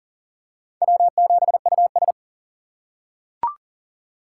Morse code, heard as a keyed tone, about 40 wpm, 700 Hz. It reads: W7US
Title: USA Call Signs - 4 Characters - 6x Character Spacing - 40wpm